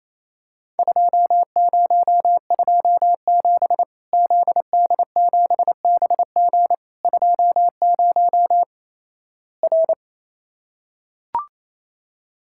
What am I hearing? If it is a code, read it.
Morse code, heard as a keyed tone, about 28 wpm, 700 Hz. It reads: 2027 ZD7BG 20 R
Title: Worked All Europe DX Contest - QTC Exchange - 28wpm